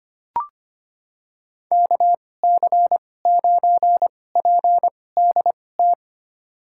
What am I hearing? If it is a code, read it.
Morse code, heard as a keyed tone, about 25 wpm, spaced as for 18 wpm, 700 Hz. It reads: KC9PDT
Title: USA Call Signs - 6 Characters - 2x Character Spacing - 25wpm